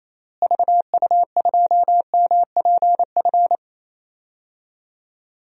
Morse code, heard as a keyed tone, about 27 wpm, 700 Hz. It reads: VU2MPF
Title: DXCC-100 Call Signs - 6 Characters - Repeated 2x - 28wpm